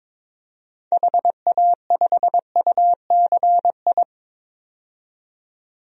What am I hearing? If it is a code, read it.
Morse code, heard as a keyed tone, about 22 wpm, 700 Hz. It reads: HA5UCI
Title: DXCC-100 Call Signs - 6 Characters - 22wpm